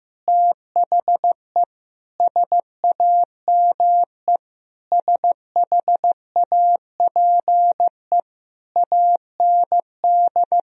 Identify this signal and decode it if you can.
Morse code, heard as a keyed tone, about 15 wpm, 700 Hz. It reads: THE SAME SHAPE AND
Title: Sentences from Top 1000 Words - Review Part 3 - 15wpm